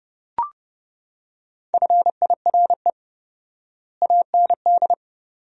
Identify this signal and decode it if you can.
Morse code, heard as a keyed tone, about 30 wpm, 700 Hz. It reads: FIRE AND
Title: 4x Word-Spacing - Sentences from Top 400 Words - 30wpm